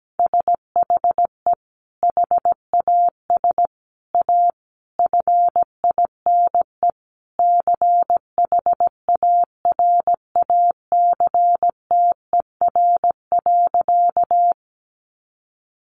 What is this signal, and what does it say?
Morse code, heard as a keyed tone, about 17 wpm, 700 Hz. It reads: SHE HAS A FINE CHARACTER.
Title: Sentences from Top 1200 Words 17wpm